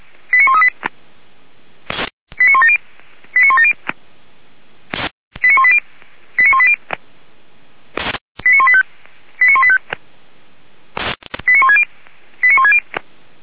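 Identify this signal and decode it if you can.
Five-tone selective call (ZVEI), heard as quick runs of five tones, tones of about 70 ms. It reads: 89138 89170 89170 89180 89180 89185 89185 89160 89160
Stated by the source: ZVEI five-tone selective call received by SDR, Feuerwehr Orlamünde.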